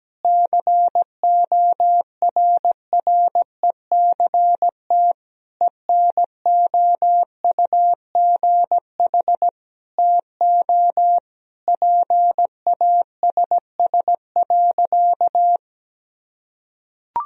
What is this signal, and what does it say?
Morse code, heard as a keyed tone, about 17 wpm, 700 Hz. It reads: CORRECT ENOUGH TO PASS.
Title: Sentences from Top 1000 Words - Review Part 2 - 17wpm